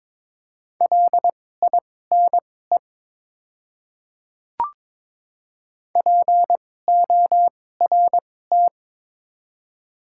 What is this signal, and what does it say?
Morse code, heard as a keyed tone, about 22 wpm, spaced as for 16 wpm, 700 Hz. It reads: LINE PORT
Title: 2x Character Spacing - Top 200 Words - 22wpm